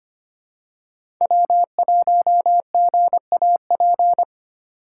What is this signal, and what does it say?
Morse code, heard as a keyed tone, about 25 wpm, 700 Hz. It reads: W1GAP
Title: USA Call Signs - 5 Characters - 2x Character Spacing - 25wpm